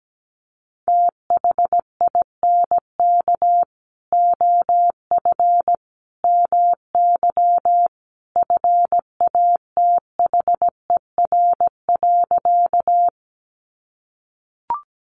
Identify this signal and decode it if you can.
Morse code, heard as a keyed tone, about 17 wpm, 700 Hz. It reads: THINK OF MY FATHER.
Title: Sentences from Top 300 Words 17wpm